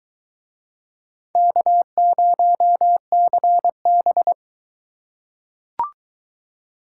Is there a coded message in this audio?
Yes: Morse code from 1.4 to 4.3 s, heard as a keyed tone, about 23 wpm, 700 Hz. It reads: K0CB